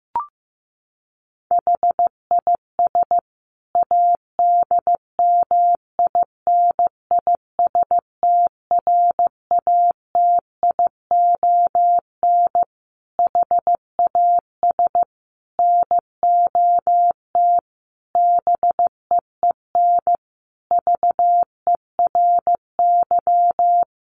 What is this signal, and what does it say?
Morse code, heard as a keyed tone, about 15 wpm, 700 Hz. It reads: HIS ADMINISTRATION HAS NOT BEEN VERY
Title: Sentences from Top 1300 Words 15wpm